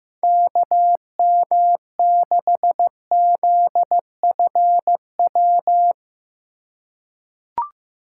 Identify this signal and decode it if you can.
Morse code, heard as a keyed tone, about 15 wpm, 700 Hz. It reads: KM6ZFW
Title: USA Call Signs - 6 Characters - 2x Character Spacing - 15wpm